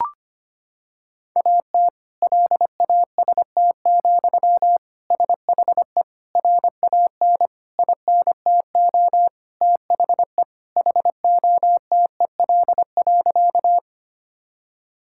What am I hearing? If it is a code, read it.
Morse code, heard as a keyed tone, about 25 wpm, 700 Hz. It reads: AT LAST, SHE RAN INTO THE HOTEL.
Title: Sentences from Top 1600 Words 25wpm